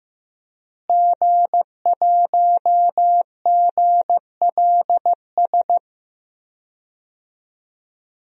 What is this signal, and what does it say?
Morse code, heard as a keyed tone, about 15 wpm, 700 Hz. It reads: G1GLS
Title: DXCC-100 Call Signs - 5 Characters - Repeated 2x - 15wpm